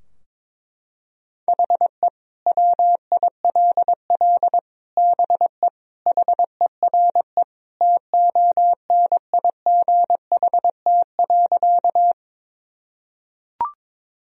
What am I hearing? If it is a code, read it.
Morse code, heard as a keyed tone, about 22 wpm, 700 Hz. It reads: HE WILL BE HERE TONIGHT.
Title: Sentences from Top 1000 Words - Review Part 3 - 22wpm